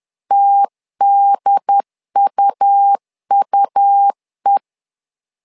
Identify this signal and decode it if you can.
Morse code, heard as a keyed tone, about 10 wpm, 800 Hz. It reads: TDUUE
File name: LEZIONI / 2 UHDN / 2 UHD 10wpm 1826